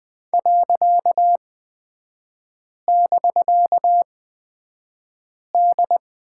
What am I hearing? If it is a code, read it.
Morse code, heard as a keyed tone, about 20 wpm, spaced as for 5 wpm, 700 Hz. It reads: .<BK>D
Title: 255 Lesson 41e - Period - Three Characters 20wpm